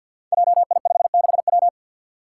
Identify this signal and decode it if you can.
Morse code, heard as a keyed tone, about 51 wpm, 700 Hz. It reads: JI56W
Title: Five Challenging Letter-Number Combinations 50wpm